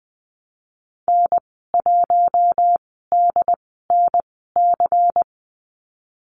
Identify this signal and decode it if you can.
Morse code, heard as a keyed tone, about 19 wpm, spaced as for 14 wpm, 700 Hz. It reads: N1DNC
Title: 173 Lesson 28g - Number 9 - Call Signs 20wpm